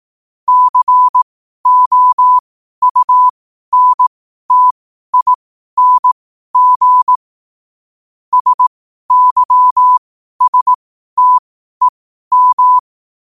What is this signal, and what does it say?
Morse code, heard as a keyed tone, about 18 wpm, spaced as for 12 wpm, 1000 Hz. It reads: COUNTING SYSTEM
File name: qotdfiles / 12wpm